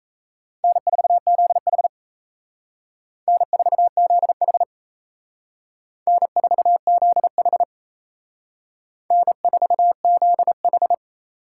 Morse code, heard as a keyed tone, about 37 wpm, 700 Hz. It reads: N4ZH N4ZH N4ZH N4ZH